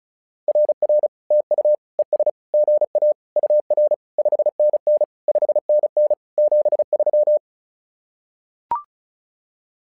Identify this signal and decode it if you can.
Morse code, heard as a keyed tone, about 36 wpm, 600 Hz. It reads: RR TU ES GA UR 5NN 5NN 73
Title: SOTA QSO Practice 35wpm